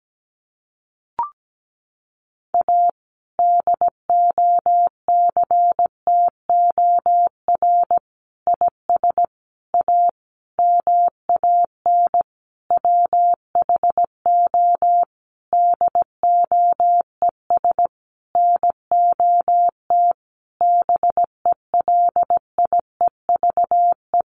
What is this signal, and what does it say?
Morse code, heard as a keyed tone, about 17 wpm, 700 Hz. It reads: A DOCTOR IS A MAN WHO DOES NOT BELIEVE
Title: Sentences from Top 1500 Words 17wpm